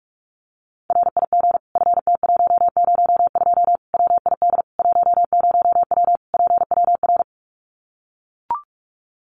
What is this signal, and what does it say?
Morse code, heard as a keyed tone, about 44 wpm, 700 Hz. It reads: RIG FT102 WID 10W PWR
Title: Common QSO Phrases 45wpm